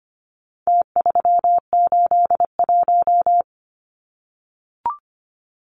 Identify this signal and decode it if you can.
Morse code, heard as a keyed tone, about 25 wpm, 700 Hz. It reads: T381
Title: Numbers - Contest Style - 4-Digit 25wpm